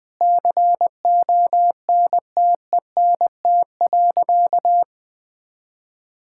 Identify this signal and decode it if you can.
Morse code, heard as a keyed tone, about 20 wpm, 700 Hz. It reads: CONTENT.
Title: Sentences from Top 1000 Words - Review Part 6 - 20wpm